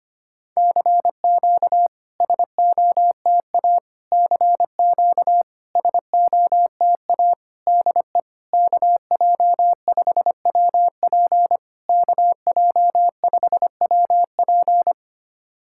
Morse code, heard as a keyed tone, about 25 wpm, 700 Hz. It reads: CQ SOTA CQ SOTA DE KJ5WP KJ5WP
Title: SOTA QSO Practice 25wpm